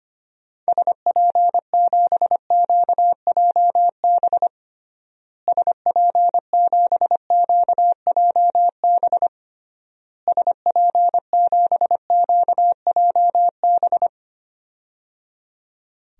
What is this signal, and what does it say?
Morse code, heard as a keyed tone, about 25 wpm, 700 Hz. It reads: SP7QJB SP7QJB SP7QJB
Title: DXCC-100 Call Signs - 6 Characters - Repeated 3x - 25wpm